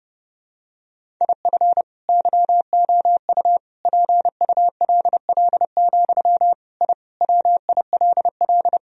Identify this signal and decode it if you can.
Morse code, heard as a keyed tone, about 30 wpm, 700 Hz. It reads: IF YOU PULL, I WILL
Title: Sentences from Top 1000 Words - Review Part 2 - 30wpm